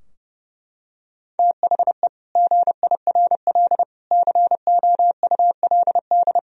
Morse code, heard as a keyed tone, about 30 wpm, 700 Hz. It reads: THE GIRL COULD
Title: Sentences from Top 1000 Words - Review Part 4 - 30wpm